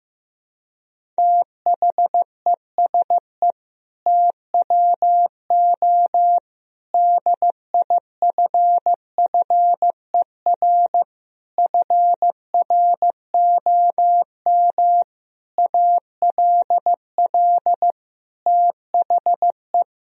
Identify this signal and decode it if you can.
Morse code, heard as a keyed tone, about 15 wpm, 700 Hz. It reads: THESE TWO DIFFER FROM ALL THE
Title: Sentences from Top 1000 Words - Review Part 1 - 15wpm